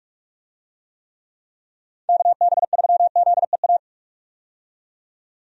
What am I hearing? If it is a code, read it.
Morse code, heard as a keyed tone, about 46 wpm, 700 Hz. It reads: KB3ZEA
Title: USA Call Signs - 6 Characters - 45wpm